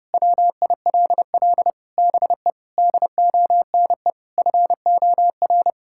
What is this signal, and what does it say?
Morse code, heard as a keyed tone, about 32 wpm, 700 Hz. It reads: WILL BE DONE FOR
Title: Sentences from Top 1600 Words 30wpm